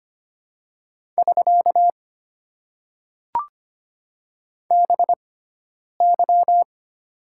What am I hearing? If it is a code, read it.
Morse code, heard as a keyed tone, about 25 wpm, spaced as for 9 wpm, 700 Hz. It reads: <SK> BY
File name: Top 90 QSO Elements - 6x Character Spacing - 25wpm